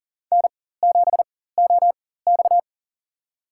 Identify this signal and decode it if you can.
Morse code, heard as a keyed tone, about 40 wpm, spaced as for 19 wpm, 700 Hz. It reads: N7OX
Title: USA Call Signs - 4x Character Spacing - 40wpm